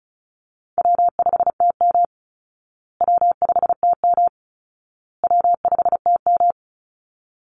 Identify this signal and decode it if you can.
Morse code, heard as a keyed tone, about 34 wpm, 700 Hz. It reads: W5TM W5TM W5TM
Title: Easy Call Signs Repeated 3x - 35wpm